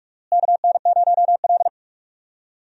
Morse code, heard as a keyed tone, about 46 wpm, 700 Hz. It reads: KN0L
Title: USA Call Signs - 4 Characters - Repeated 2x - 45wpm